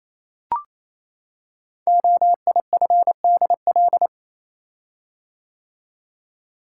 Morse code, heard as a keyed tone, about 29 wpm, 700 Hz. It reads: OIFDL
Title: Code Groups - 5 Letters - 28wpm